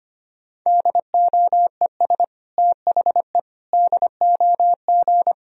Morse code, heard as a keyed tone, about 25 wpm, 700 Hz. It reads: DOES THE DOG
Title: Sentences from Top 1000 Words - Review Part 3 - 25wpm